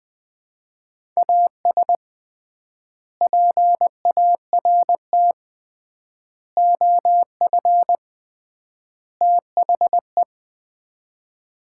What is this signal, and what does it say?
Morse code, heard as a keyed tone, about 21 wpm, 700 Hz. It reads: AS PART OF THE
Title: 3x Word-Spacing - Sets of 4 Words - 20wpm